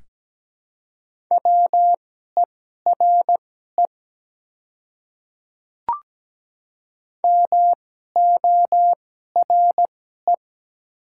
Morse code, heard as a keyed tone, about 17 wpm, spaced as for 12 wpm, 700 Hz. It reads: WERE MORE
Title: 2x Character Spacing - Top 300 Words - 17wpm